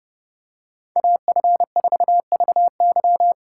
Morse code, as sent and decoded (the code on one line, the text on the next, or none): .- ..-. ....- ...- -.--
AF4VY